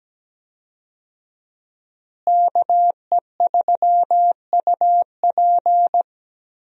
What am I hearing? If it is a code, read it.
Morse code, heard as a keyed tone, about 17 wpm, 700 Hz. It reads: KE3UP
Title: USA Call Signs - 5 Characters - Repeated 3x - 17wpm